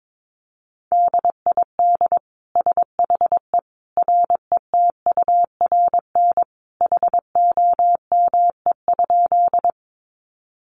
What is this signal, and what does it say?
Morse code, heard as a keyed tone, about 22 wpm, 700 Hz. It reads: DID SHE RETURN HOME?
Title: Sentences from Top 800 Words 22wpm